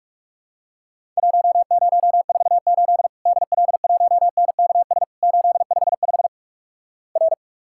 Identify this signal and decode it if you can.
Morse code, heard as a keyed tone, about 44 wpm, 700 Hz. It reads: 1048 DL1NKS 855 R
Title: Worked All Europe DX Contest - QTC Exchange - 45wpm